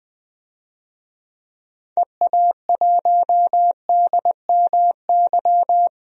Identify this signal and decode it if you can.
Morse code, heard as a keyed tone, about 21 wpm, 700 Hz. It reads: EA1DMY
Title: DXCC-100 Call Signs - 6 Characters - 20wpm